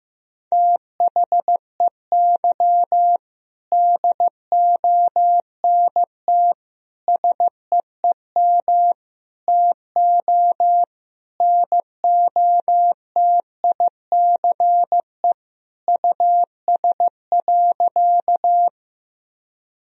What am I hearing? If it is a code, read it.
Morse code, heard as a keyed tone, about 15 wpm, 700 Hz. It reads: THEY DONT SEEM TO NOTICE US.
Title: Sentences from Top 1000 Words - Review Part 2 - 15wpm